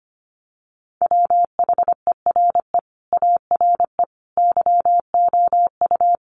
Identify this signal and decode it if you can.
Morse code, heard as a keyed tone, about 25 wpm, 690 Hz. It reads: WHERE ARE YOU
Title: Sentences from Top 1100 Words 25wpm